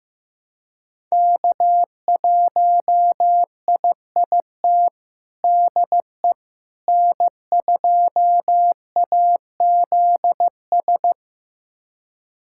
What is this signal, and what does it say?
Morse code, heard as a keyed tone, about 15 wpm, 700 Hz. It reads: K1IIT DE N2AZS
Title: USA Callsign DE Callsign - 5 Characters - 15wpm